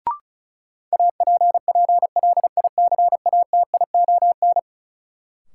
Morse code, heard as a keyed tone, about 36 wpm, 700 Hz. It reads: APPLICATION
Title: Sentences from Top 1200 Words - New Word Drill Alt - 35wpm